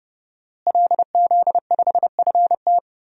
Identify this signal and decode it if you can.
Morse code, heard as a keyed tone, about 30 wpm, 700 Hz. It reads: LZ5FT
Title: DXCC-100 Call Signs - 5 Characters - Repeated 2x - 30wpm